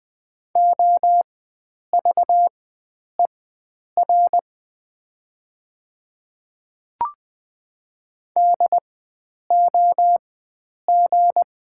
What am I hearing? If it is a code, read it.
Morse code, heard as a keyed tone, about 20 wpm, spaced as for 9 wpm, 690 Hz. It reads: OVER DOG
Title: 4x Character Spacing - Top 500 Words - 20wpm